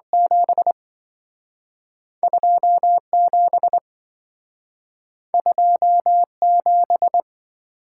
Morse code, heard as a keyed tone, about 27 wpm, 700 Hz. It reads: 7 27 27